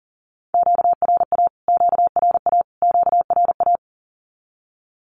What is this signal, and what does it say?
Morse code, heard as a keyed tone, about 40 wpm, 700 Hz. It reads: QRA QRA QRA
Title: Q-Codes Repeated 3x 40wpm